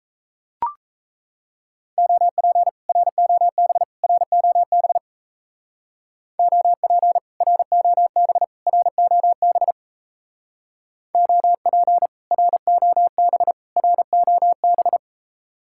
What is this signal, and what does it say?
Morse code, heard as a keyed tone, about 33 wpm, 700 Hz. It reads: OP ROB ROB OP ROB ROB OP ROB ROB
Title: Speed-Racing Common QSO Phrases - 28wpm